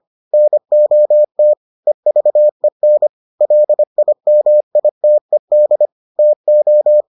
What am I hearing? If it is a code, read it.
Morse code, heard as a keyed tone, about 25 wpm, 600 Hz. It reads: NOT EVEN LIMITED TO